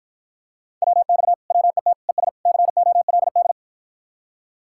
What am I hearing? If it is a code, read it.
Morse code, heard as a keyed tone, about 52 wpm, 700 Hz. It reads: WX WET ES COLD